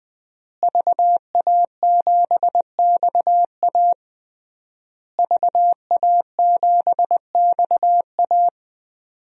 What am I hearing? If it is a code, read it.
Morse code, heard as a keyed tone, about 20 wpm, 700 Hz. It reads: VA7XA VA7XA
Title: DXCC-100 Call Signs - 5 Characters - Repeated 3x - 20wpm